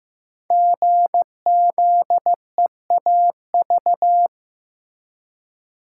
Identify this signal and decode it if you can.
Morse code, heard as a keyed tone, about 15 wpm, 700 Hz. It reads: GZEAV
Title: Code Groups - 5 Letter:Numbers - 15wpm